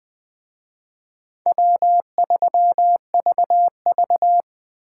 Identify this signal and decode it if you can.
Morse code, heard as a keyed tone, about 21 wpm, 700 Hz. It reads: W3VV